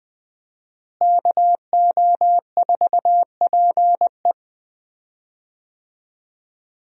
Morse code, heard as a keyed tone, about 20 wpm, 700 Hz. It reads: KO4PE
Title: USA Call Signs - 5 Characters - Repeated 2x - 20wpm